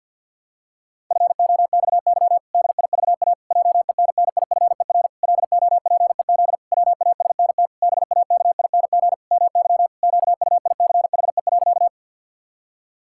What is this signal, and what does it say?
Morse code, heard as a keyed tone, about 50 wpm, 700 Hz. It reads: FOXY DIVA JENNIFER LOPEZ WASNT BAKING MY QUICHE.
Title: Pangram Challenge v2 - 50wpm